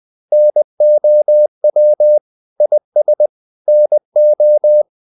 Morse code, heard as a keyed tone, about 20 wpm, 600 Hz. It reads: NOW IS NO
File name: the old man and the sea / the old man and the sea part 1 20wpm